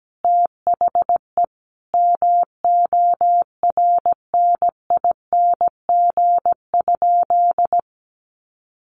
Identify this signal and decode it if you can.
Morse code, heard as a keyed tone, about 17 wpm, 700 Hz. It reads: THE MORNING?
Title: Sentences from Top 1300 Words 17wpm